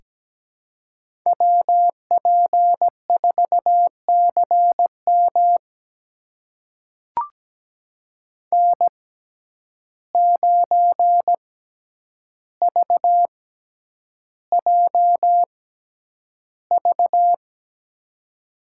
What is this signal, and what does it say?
Morse code, heard as a keyed tone, about 17 wpm, spaced as for 9 wpm, 700 Hz. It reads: WP4CM N9VJV
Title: USA Call Signs - 5 Characters - 6x Character Spacing - 17wpm